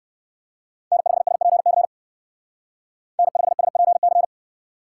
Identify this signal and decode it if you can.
Morse code, heard as a keyed tone, about 56 wpm, 700 Hz. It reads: N5SPC N5SPC
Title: Speed Racing - USA 1x3 Call Signs 45wpm